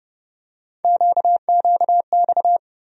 Morse code, heard as a keyed tone, about 30 wpm, 700 Hz. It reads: QQX
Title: USA Call Sign Postfixes - 3 Letters 30wpm